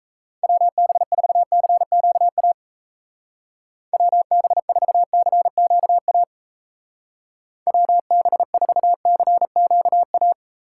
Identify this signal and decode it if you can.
Morse code, heard as a keyed tone, about 34 wpm, 700 Hz. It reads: WB4CQA WB4CQA WB4CQA
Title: Speed Racing - USA 2x3 Call Signs 28wpm